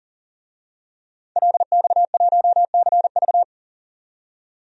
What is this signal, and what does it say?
Morse code, heard as a keyed tone, about 37 wpm, 700 Hz. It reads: LX1CV
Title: DXCC-100 Call Signs - 5 Characters - 40wpm